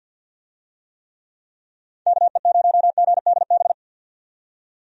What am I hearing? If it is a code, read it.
Morse code, heard as a keyed tone, about 50 wpm, 700 Hz. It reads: KE0GDB
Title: USA Call Signs - 6 Characters - 50wpm